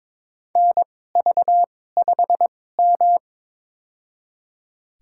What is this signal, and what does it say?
Morse code, heard as a keyed tone, about 22 wpm, spaced as for 16 wpm, 700 Hz. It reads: NV5M
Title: USA Call Signs - 4 Characters - 2x Character Spacing - 22wpm